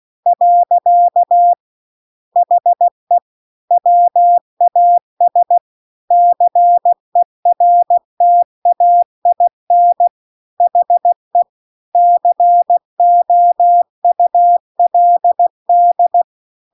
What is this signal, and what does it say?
Morse code, heard as a keyed tone, about 16 wpm, 700 Hz. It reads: . HE WAS CERTAIN HE COULD
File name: CW-The-Big-Twoheaded-River-2-16WPM